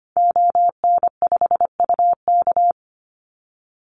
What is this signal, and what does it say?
Morse code, heard as a keyed tone, about 25 wpm, 700 Hz. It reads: ON5UK
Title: Call Signs - 25wpm